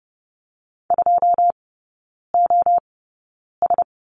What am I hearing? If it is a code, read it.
Morse code, heard as a keyed tone, about 30 wpm, spaced as for 9 wpm, 700 Hz. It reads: 2OS
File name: 081 Lesson 14e - Number 2 - Three Characters 30wpm